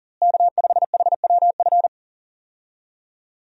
Morse code, heard as a keyed tone, about 38 wpm, 700 Hz. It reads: K5HWF
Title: USA Call Signs - 5 Characters - Repeated 3x - 40wpm